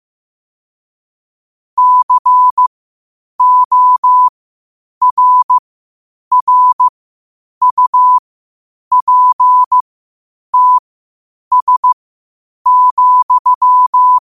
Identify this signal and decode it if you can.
Morse code, heard as a keyed tone, about 15 wpm, spaced as for 8 wpm, 1000 Hz. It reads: CORRUPTS,